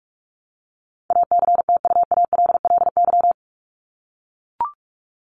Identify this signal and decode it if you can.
Morse code, heard as a keyed tone, about 44 wpm, 700 Hz. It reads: ACTUALLY
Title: Top 501-750 Words 45wpm